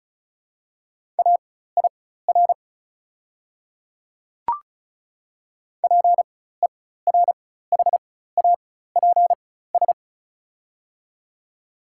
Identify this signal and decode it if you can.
Morse code, heard as a keyed tone, about 36 wpm, spaced as for 16 wpm, 690 Hz. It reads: AIR PERHAPS
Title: 4x Character Spacing - Top 500 Words - 35wpm